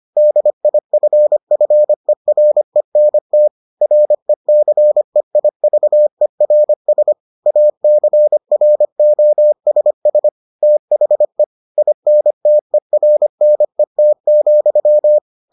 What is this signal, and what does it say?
Morse code, heard as a keyed tone, about 25 wpm, 600 Hz. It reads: DIFFERENT RECEIVERS ACROSS THE INTERNET,